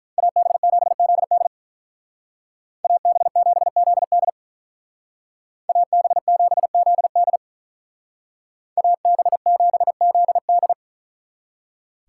Morse code, heard as a keyed tone, about 43 wpm, 700 Hz. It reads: AB7ZD AB7ZD AB7ZD AB7ZD
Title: Speed Racing - USA 2x2 Call Signs 35wpm